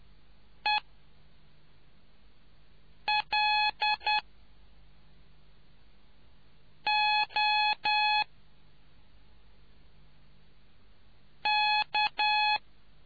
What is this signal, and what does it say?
Morse code, heard as a keyed tone, about 10 wpm, spaced as for 3 wpm, 850 Hz. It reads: ELOK